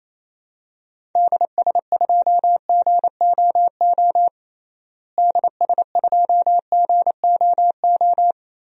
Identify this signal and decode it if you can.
Morse code, heard as a keyed tone, about 28 wpm, 700 Hz. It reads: DS2GOO DS2GOO
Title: DXCC-100 Call Signs - 6 Characters - Repeated 3x - 28wpm